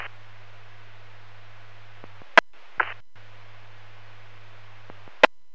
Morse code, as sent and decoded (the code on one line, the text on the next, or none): none